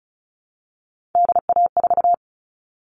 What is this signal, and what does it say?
Morse code, heard as a keyed tone, about 34 wpm, 700 Hz. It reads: DA4
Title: All Cut Numbers - 3-Digit 35wpm